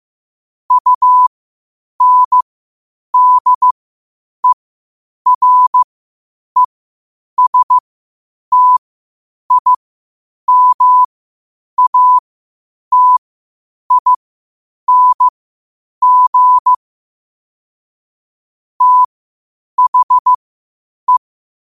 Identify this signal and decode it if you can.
Morse code, heard as a keyed tone, about 15 wpm, spaced as for 8 wpm, 1000 Hz. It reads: UNDERESTIMATING THE
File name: qotdfiles / 8wpm